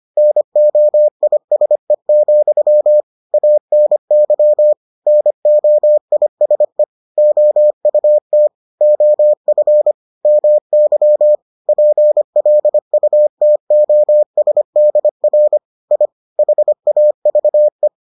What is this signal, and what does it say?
Morse code, heard as a keyed tone, about 25 wpm, 600 Hz. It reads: NOISE, ANY NOISE OUT OF MY PLUTOSDR I HAVE